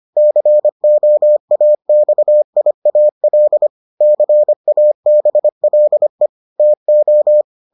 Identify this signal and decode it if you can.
Morse code, heard as a keyed tone, about 25 wpm, 600 Hz. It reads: COAXIAL CABLE TO